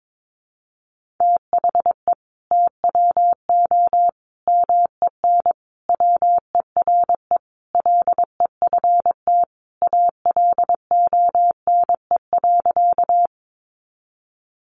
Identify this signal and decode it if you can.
Morse code, heard as a keyed tone, about 22 wpm, 700 Hz. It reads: THE TWO MEN WERE LEFT ALONE.
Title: Sentences from Top 1000 Words 22wpm